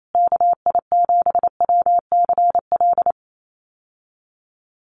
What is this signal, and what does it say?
Morse code, heard as a keyed tone, about 28 wpm, 700 Hz. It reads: KI7WCL
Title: USA Call Signs Repeated 3x 28wpm